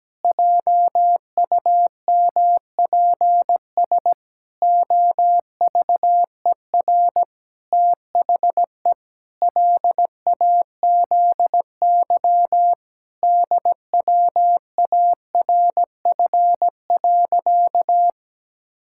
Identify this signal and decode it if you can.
Morse code, heard as a keyed tone, about 17 wpm, 700 Hz. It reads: JUMPS OVER THE LAZY DWARF.
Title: Pangram Challenge v2 - 17wpm